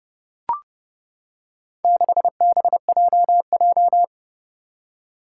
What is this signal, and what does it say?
Morse code, heard as a keyed tone, about 30 wpm, 700 Hz. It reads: 6BJJ
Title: Four Challenging Letter-Number Combinations 30wpm